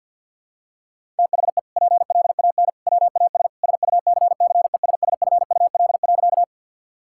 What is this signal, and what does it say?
Morse code, heard as a keyed tone, about 52 wpm, 700 Hz. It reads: THE PLAN WAS SUCCESSFUL.
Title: Sentences from Top 1000 Words - Review Part 6 - 50wpm